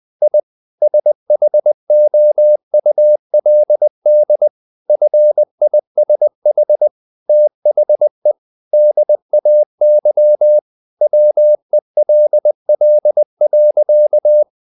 Morse code, heard as a keyed tone, about 20 wpm, 600 Hz. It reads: I SHOULD FISH THE DAY WELL.
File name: the old man and the sea / the old man and the sea part 1 20wpm